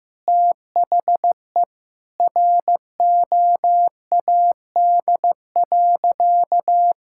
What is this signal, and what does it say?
Morse code, heard as a keyed tone, about 16 wpm, 700 Hz. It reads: THE ROAD.
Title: Sentences from Top 1000 Words - Review Part 5 - 15wpm